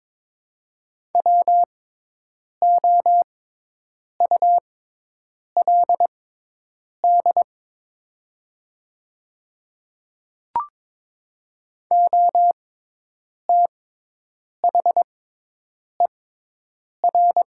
Morse code, heard as a keyed tone, about 22 wpm, spaced as for 8 wpm, 700 Hz. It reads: WOULD OTHER
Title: 6x Character Spacing - Top 200 Words - 22wpm